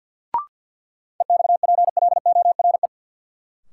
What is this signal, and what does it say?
Morse code, heard as a keyed tone, about 52 wpm, 700 Hz. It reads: EXPLORE
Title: Sentences from Top 1100 Words - New Word Drill Alt - 50wpm